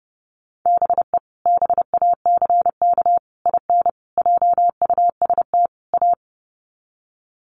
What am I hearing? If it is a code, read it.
Morse code, heard as a keyed tone, about 30 wpm, 700 Hz. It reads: BE BACK IN JUST A
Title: Sets of 5 Words Encore 30wpm